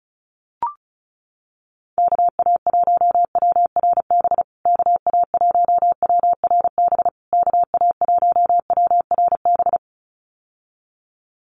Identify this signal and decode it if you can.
Morse code, heard as a keyed tone, about 34 wpm, 700 Hz. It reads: KA1WRB KA1WRB KA1WRB
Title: USA Call Signs Repeated 3x 35wpm